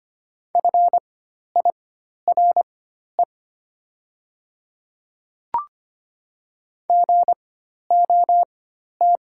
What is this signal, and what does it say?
Morse code, heard as a keyed tone, about 25 wpm, spaced as for 12 wpm, 700 Hz. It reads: FIRE GOT
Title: 4x Character Spacing - Top 500 Words - 25wpm